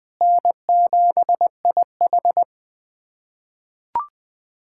Morse code, heard as a keyed tone, about 21 wpm, 700 Hz. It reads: N7IH
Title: USA Callsign DE Callsign - 4 Characters - 20wpm